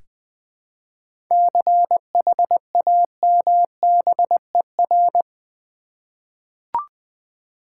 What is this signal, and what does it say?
Morse code, heard as a keyed tone, about 20 wpm, 700 Hz. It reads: CHAMBER
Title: 7 Letter Words - First 3 Letters Given - 20wpm